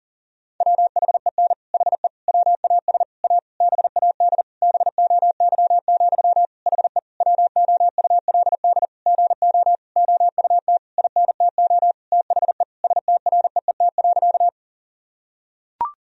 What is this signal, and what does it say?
Morse code, heard as a keyed tone, about 40 wpm, 700 Hz. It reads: WHEN HE WAS A BAD BOY, HE WOULD GO OUT INTO THE STREET.
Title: Sentences from Top 1000 Words - Review Part 3 - 40wpm